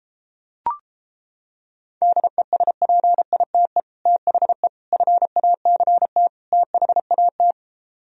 Morse code, heard as a keyed tone, about 29 wpm, 700 Hz. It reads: DESPITE THE FACT THAT